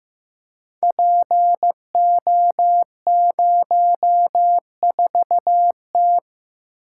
Morse code, heard as a keyed tone, about 15 wpm, 700 Hz. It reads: PO04T
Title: Code Groups - 5 Letter:Numbers - 15wpm